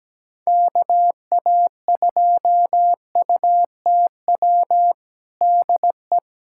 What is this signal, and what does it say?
Morse code, heard as a keyed tone, about 17 wpm, 700 Hz. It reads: KA2UTW DE
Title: USA Callsign DE Callsign 17wpm